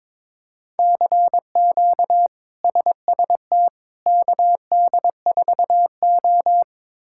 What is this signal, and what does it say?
Morse code, heard as a keyed tone, about 22 wpm, 700 Hz. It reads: CQ SST KD4O
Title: K1USN SST QSO Practice - 22wpm